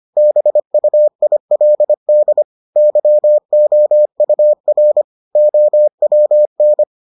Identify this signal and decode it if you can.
Morse code, heard as a keyed tone, about 25 wpm, 600 Hz. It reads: BUILD YOUR OWN